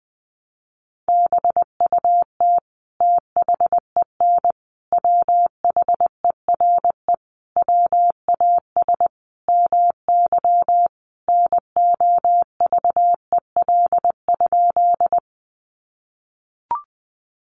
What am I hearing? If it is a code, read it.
Morse code, heard as a keyed tone, about 20 wpm, 700 Hz. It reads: BUT THEN WHERE WAS MY NOVEL?
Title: Sentences from Top 1300 Words 20wpm